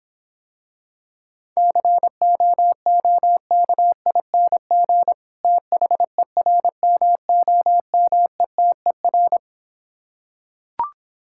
Morse code, heard as a keyed tone, about 26 wpm, 700 Hz. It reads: COOKING THERMOMETER